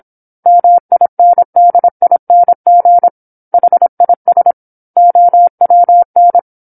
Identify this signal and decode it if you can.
Morse code, heard as a keyed tone, about 27 wpm, 700 Hz. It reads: MINDING HIS OWN
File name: 08-Chapter-08 26 WPM